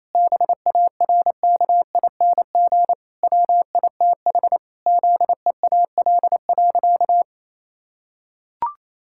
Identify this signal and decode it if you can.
Morse code, heard as a keyed tone, about 28 wpm, 700 Hz. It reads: BARKING WITH ZEAL.
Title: Pangram Challenge v2 - 28wpm